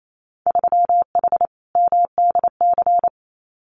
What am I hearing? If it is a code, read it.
Morse code, heard as a keyed tone, about 28 wpm, 700 Hz. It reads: 3H MDC
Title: Winter Field Day Exchanges Encore 28wpm